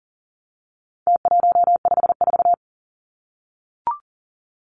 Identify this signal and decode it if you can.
Morse code, heard as a keyed tone, about 40 wpm, 700 Hz. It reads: T154
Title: Numbers - Contest Style - 4-Digit 40wpm